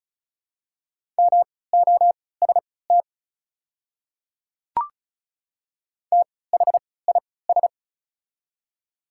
Morse code, heard as a keyed tone, about 35 wpm, spaced as for 20 wpm, 690 Hz. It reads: MOST THIS
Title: 3x Character Spacing - Top 100 Words - 35wpm